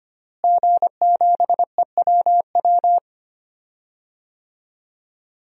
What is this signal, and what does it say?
Morse code, heard as a keyed tone, about 25 wpm, 700 Hz. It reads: G7EWW
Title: DXCC-100 Call Signs - 5 Characters - Repeated 2x - 25wpm